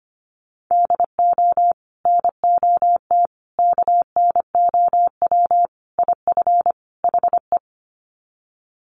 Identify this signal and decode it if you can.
Morse code, heard as a keyed tone, about 25 wpm, 700 Hz. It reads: DO NOT KNOW IF HE
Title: Sets of 5 Words Encore2 25wpm